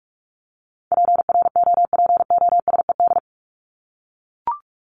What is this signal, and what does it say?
Morse code, heard as a keyed tone, about 44 wpm, 700 Hz. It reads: PROPOSED
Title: ED Suffix - Top 100 Words with Base Words 45wpm